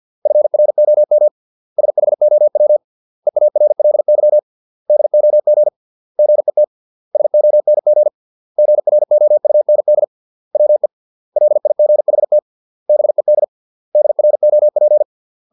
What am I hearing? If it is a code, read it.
Morse code, heard as a keyed tone, about 50 wpm, 600 Hz. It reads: FROM SHOW EARLY DOG GET SONG GROUND WE LIGHT BED DROP